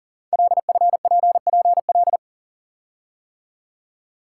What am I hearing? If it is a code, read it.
Morse code, heard as a keyed tone, about 42 wpm, 690 Hz. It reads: LFPPL
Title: Code Groups - 5 Letters - 40wpm